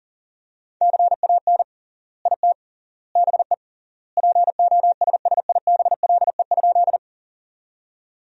Morse code, heard as a keyed tone, about 37 wpm, 700 Hz. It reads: CAN IT BE POSSIBLE?
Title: 3x Word-Spacing - Sentences from Top 500 Words - 40wpm